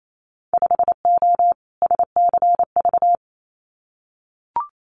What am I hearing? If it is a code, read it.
Morse code, heard as a keyed tone, about 28 wpm, 700 Hz. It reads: 5O SCV
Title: Winter Field Day Exchanges Encore 28wpm